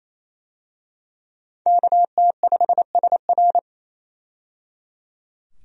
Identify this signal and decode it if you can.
Morse code, heard as a keyed tone, about 28 wpm, 700 Hz. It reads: KT5SR
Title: USA Call Signs - 5 Characters - 28wpm